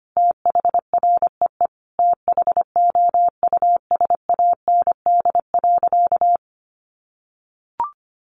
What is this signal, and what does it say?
Morse code, heard as a keyed tone, about 25 wpm, 700 Hz. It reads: THREE THOUSAND.
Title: Sentences from Top 1300 Words 25wpm